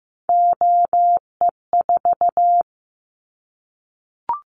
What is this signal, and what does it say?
Morse code, heard as a keyed tone, about 15 wpm, 700 Hz. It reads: OE4
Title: Callsign Prefixes for Top 100 DXCC - 15wpm